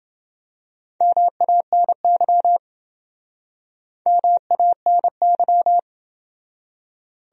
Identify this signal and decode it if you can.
Morse code, heard as a keyed tone, about 27 wpm, 700 Hz. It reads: MANY MANY